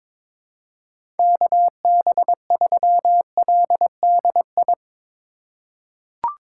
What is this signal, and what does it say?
Morse code, heard as a keyed tone, about 22 wpm, 700 Hz. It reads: KB3LDI
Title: USA Call Signs - 6 Characters - 3x Character Spacing - 22wpm